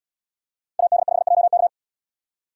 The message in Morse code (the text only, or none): AD5JK